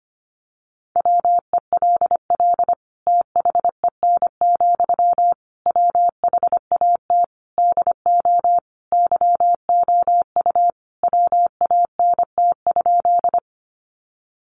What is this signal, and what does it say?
Morse code, heard as a keyed tone, about 25 wpm, 700 Hz. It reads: WELL THEN, WHAT DO YOU WANT?
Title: Sentences from Top 200 Words 25wpm